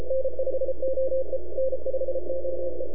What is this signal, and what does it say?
Morse code, heard as a keyed tone, about 32 wpm, 500 Hz. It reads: N5OE N5OE
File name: NAQPCW-8-2009 / W1UJ NAQP CW 09 2300Z